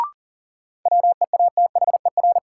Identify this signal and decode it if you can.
Morse code, heard as a keyed tone, about 40 wpm, 700 Hz. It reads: WEATHER
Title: Sentences from Top 800 Words - New Word Drill - 40wpm